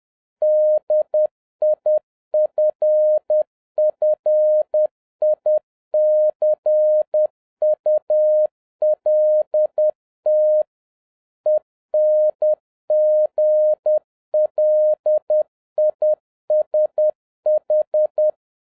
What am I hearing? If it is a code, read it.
Morse code, heard as a keyed tone, about 10 wpm, 610 Hz. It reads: DIFFICULT ENGLISH
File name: 2011-07-mp3 / n56 BACK FROM A BIG ACCIDENT 10wpm